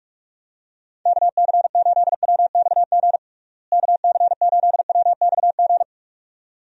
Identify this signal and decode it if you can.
Morse code, heard as a keyed tone, about 46 wpm, 700 Hz. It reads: KC8WXG KC8WXG
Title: USA Call Signs - 6 Characters - Repeated 2x - 45wpm